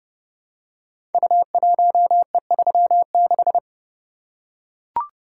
Morse code, heard as a keyed tone, about 30 wpm, 700 Hz. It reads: U1E36
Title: Code Groups - 5 Letter:Numbers - 30wpm